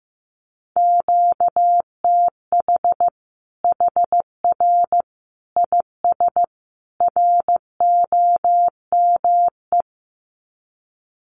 Morse code, heard as a keyed tone, about 15 wpm, 700 Hz. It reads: QTH HR IS ROME
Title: Common QSO Phrases 15wpm